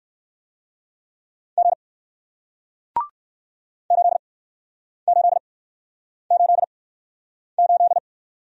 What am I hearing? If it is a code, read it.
Morse code, heard as a keyed tone, about 56 wpm, spaced as for 9 wpm, 700 Hz. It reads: G 8888
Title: Speed-Racing Single Letter-Number 45wpm